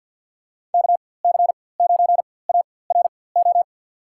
Morse code, heard as a keyed tone, about 51 wpm, spaced as for 23 wpm, 690 Hz. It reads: KC9ARO
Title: USA Call Signs - 4x Character Spacing - 50wpm